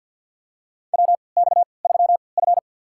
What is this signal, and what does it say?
Morse code, heard as a keyed tone, about 53 wpm, spaced as for 29 wpm, 700 Hz. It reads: WX3F
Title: USA Call Signs - 4 Characters - 3x Character Spacing - 50wpm